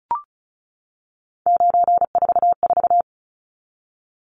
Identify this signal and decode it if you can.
Morse code, heard as a keyed tone, about 34 wpm, 700 Hz. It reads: 944
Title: Numbers - 3-Digit 35wpm